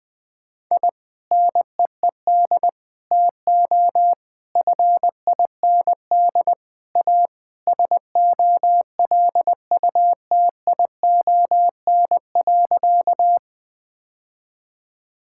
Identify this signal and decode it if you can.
Morse code, heard as a keyed tone, about 21 wpm, 700 Hz. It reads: I NEED TO FIND A SOLUTION.
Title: Sentences with 3-Word Phrases - 20wpm